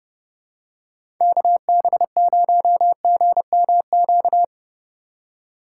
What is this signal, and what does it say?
Morse code, heard as a keyed tone, about 30 wpm, 700 Hz. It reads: KB0GMQ
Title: USA Call Signs - 4x Character Spacing - 30wpm